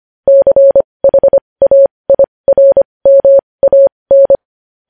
Morse code, heard as a keyed tone, about 25 wpm, spaced as for 20 wpm, 550 Hz. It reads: CHAIRMAN
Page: News Headlines in Morse Code at 20 WPM